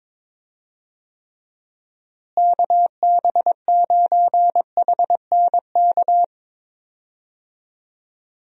Morse code, heard as a keyed tone, about 22 wpm, 700 Hz. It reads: KB9HNK
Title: USA Call Signs - 6 Characters - Repeated 3x - 22wpm